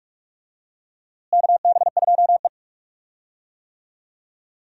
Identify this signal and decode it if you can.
Morse code, heard as a keyed tone, about 46 wpm, 700 Hz. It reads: KB2E
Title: USA Call Signs - 4 Characters - Repeated 3x - 45wpm